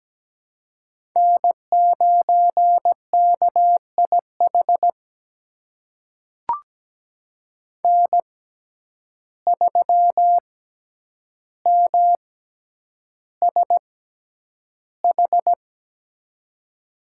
Morse code, heard as a keyed tone, about 17 wpm, spaced as for 9 wpm, 700 Hz. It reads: N9KIH N3MSH
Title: USA Call Signs - 5 Characters - 6x Character Spacing - 17wpm